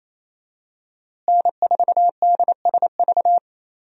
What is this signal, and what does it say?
Morse code, heard as a keyed tone, about 28 wpm, 700 Hz. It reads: N4DSV
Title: USA Call Signs - 5 Characters - Repeated 2x - 28wpm